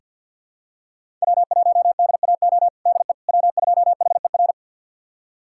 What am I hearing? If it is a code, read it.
Morse code, heard as a keyed tone, about 48 wpm, 700 Hz. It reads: W1DAO DE W2HER
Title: USA Callsign DE Callsign - 5 Characters - 50wpm